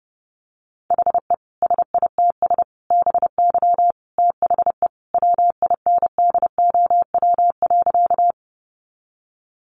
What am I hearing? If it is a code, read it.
Morse code, heard as a keyed tone, about 30 wpm, 700 Hz. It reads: HE SITS BY THE WINDOW.
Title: Easy Phrases 30wpm